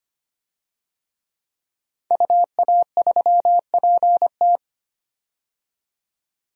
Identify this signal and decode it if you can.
Morse code, heard as a keyed tone, about 25 wpm, 700 Hz. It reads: UA3PT
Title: DXCC-100 Call Signs - 5 Characters - 25wpm